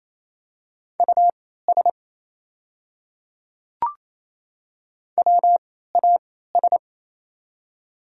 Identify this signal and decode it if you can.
Morse code, heard as a keyed tone, about 27 wpm, spaced as for 16 wpm, 700 Hz. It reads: US WAS
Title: 3x Character Spacing - Top 300 Words - 28wpm